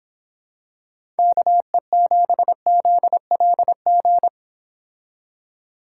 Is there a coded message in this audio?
Yes: Morse code from 1.2 to 4.3 s, heard as a keyed tone, about 25 wpm, 700 Hz. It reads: KE7ZLG